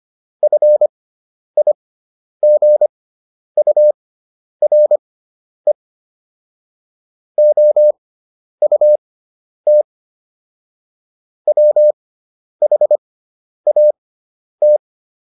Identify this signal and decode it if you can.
Morse code, heard as a keyed tone, about 25 wpm, spaced as for 10 wpm, 600 Hz. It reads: FIGURE OUT WHAT